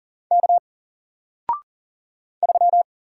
Morse code, heard as a keyed tone, about 40 wpm, 700 Hz. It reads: K 3
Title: Single Letter-Number Alt 40wpm